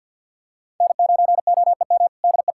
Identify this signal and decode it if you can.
Morse code, heard as a keyed tone, about 46 wpm, 690 Hz. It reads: N9OEM DE
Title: USA Callsign DE Callsign - 5 Characters - 50wpm